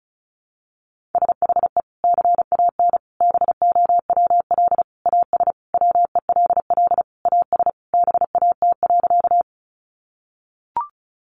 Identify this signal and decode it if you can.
Morse code, heard as a keyed tone, about 34 wpm, 700 Hz. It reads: SHE CAN BOWL AS WELL AS BAT.
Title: Sentences from Top 1500 Words 35wpm